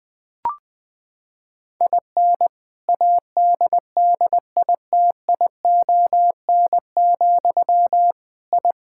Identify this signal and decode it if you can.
Morse code, heard as a keyed tone, about 20 wpm, 700 Hz. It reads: IN ADDITION, I
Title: Sentences from Top 1000 Words - Review Part 6 - 20wpm